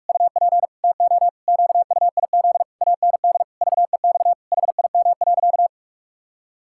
Morse code, heard as a keyed tone, about 41 wpm, 700 Hz. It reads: UP TO QUIZ AND VEX HIM.
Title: Pangram Challenge v2 - 45wpm